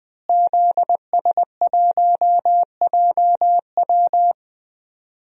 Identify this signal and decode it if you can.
Morse code, heard as a keyed tone, about 20 wpm, 700 Hz. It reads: ZS1JW
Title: DXCC-100 Call Signs - 5 Characters - Repeated 3x - 20wpm